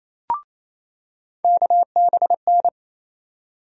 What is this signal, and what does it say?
Morse code, heard as a keyed tone, about 28 wpm, 700 Hz. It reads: KBN
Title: USA Call Sign Postfixes - 3 Letters 28wpm